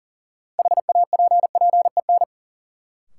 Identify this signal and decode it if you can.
Morse code, heard as a keyed tone, about 40 wpm, 700 Hz. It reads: HAPPEN
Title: Sentences from Top 400 Words - New Word Drill - 40wpm